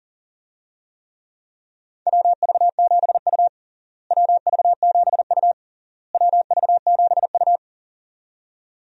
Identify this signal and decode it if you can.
Morse code, heard as a keyed tone, about 37 wpm, 700 Hz. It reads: WV7U WV7U WV7U
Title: USA Call Signs - 4 Characters - Repeated 3x - 40wpm